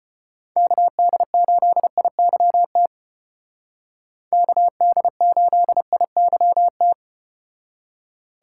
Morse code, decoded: KD8IYT KD8IYT

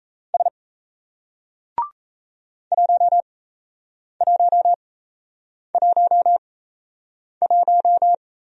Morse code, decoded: S 1111

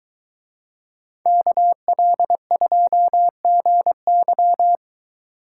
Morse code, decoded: KL2GY

